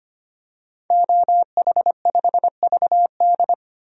Morse code, heard as a keyed tone, about 25 wpm, 700 Hz. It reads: OH5VD
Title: DXCC-100 Call Signs - 5 Characters - 25wpm